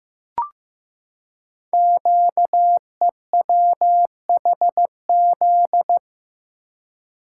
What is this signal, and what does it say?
Morse code, heard as a keyed tone, about 15 wpm, 700 Hz. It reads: QEWHZ
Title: Code Groups - 5 Letter:Numbers - 15wpm